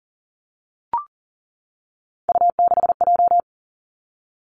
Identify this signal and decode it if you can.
Morse code, heard as a keyed tone, about 40 wpm, 700 Hz. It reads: U6J
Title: Three Challenging Letter-Number Combinations 40wpm